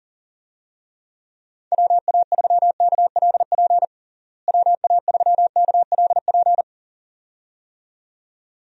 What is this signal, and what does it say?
Morse code, heard as a keyed tone, about 40 wpm, 700 Hz. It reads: WA3KLP WA3KLP
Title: USA Call Signs - 6 Characters - Repeated 2x - 40wpm